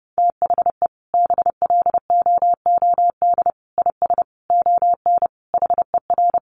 Morse code, heard as a keyed tone, about 30 wpm, 700 Hz. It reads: THE BLOOD IS ON HER
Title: Sentences from Top 1100 Words 30wpm